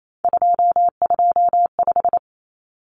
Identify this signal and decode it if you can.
Morse code, heard as a keyed tone, about 28 wpm, 700 Hz. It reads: 225
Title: Common Cut Numbers - 3-Digit 28wpm